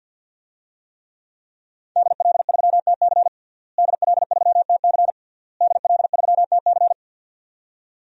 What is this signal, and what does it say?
Morse code, heard as a keyed tone, about 52 wpm, 700 Hz. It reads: DL3TC DL3TC DL3TC
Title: DXCC-100 Call Signs - 5 Characters - Repeated 3x - 50wpm